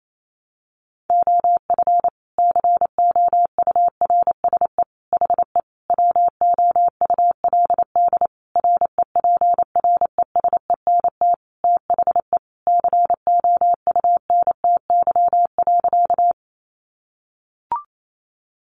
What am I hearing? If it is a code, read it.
Morse code, heard as a keyed tone, about 28 wpm, 700 Hz. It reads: OF COURSE HE WOULD REPRESENT THE COUNTY.
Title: Sentences from Top 1200 Words 28wpm